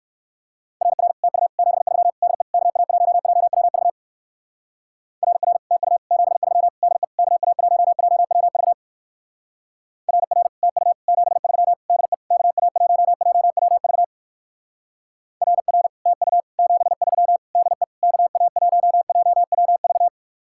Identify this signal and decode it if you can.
Morse code, heard as a keyed tone, about 55 wpm, 690 Hz. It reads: RR TU 73 DE KA1JWV RR TU 73 DE KA1JWV RR TU 73 DE KA1JWV RR TU 73 DE KA1JWV
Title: Speed-Racing Common QSO Phrases - 45wpm